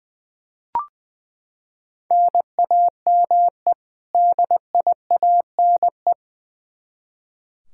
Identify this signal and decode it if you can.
Morse code, heard as a keyed tone, about 20 wpm, 700 Hz. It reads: NAME DIANE
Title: K1USN SST QSO Practice 20wpm